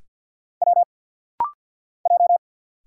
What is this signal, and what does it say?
Morse code, heard as a keyed tone, about 50 wpm, 690 Hz. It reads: W J
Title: Single Letters - Rapid-Fire 50wpm